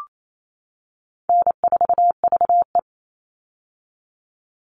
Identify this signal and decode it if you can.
Morse code, heard as a keyed tone, about 28 wpm, 700 Hz. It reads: N4VE
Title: All Cut Numbers - 4-Digit 28wpm